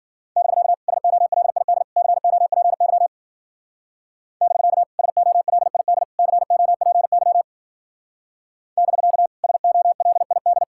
Morse code, heard as a keyed tone, about 49 wpm, 700 Hz. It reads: <BK> SOLID COPY <BK> SOLID COPY <BK> SOLID
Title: Speed-Racing Common QSO Phrases Encore - 40wpm